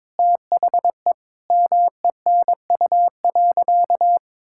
Morse code, heard as a keyed tone, about 22 wpm, 700 Hz. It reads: THE MENU.
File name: practice / 22wpm / Sentences with 4-Word Phrases - 22wpm